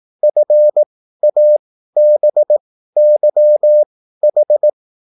Morse code, heard as a keyed tone, about 18 wpm, spaced as for 13 wpm, 600 Hz. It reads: FABYH
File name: QSL / CW lesson